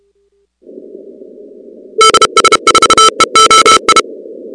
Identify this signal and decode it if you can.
Morse code, heard as a keyed tone, about 31 wpm, 1320 Hz. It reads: DS4EOI
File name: contests / 2019-cw-wpx-kt5j-k5pi- -2330 0-37